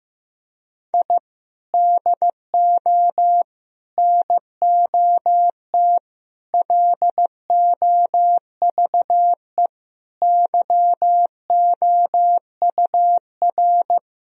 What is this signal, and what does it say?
Morse code, heard as a keyed tone, about 16 wpm, 700 Hz. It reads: I DO NOT LOVE YOUR
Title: Sentences from Top 1000 Words - Review Part 4 - 15wpm